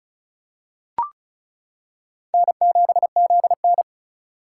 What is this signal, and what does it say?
Morse code, heard as a keyed tone, about 36 wpm, 700 Hz. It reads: N7ZN
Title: K1USN SST QSO Practice - 35wpm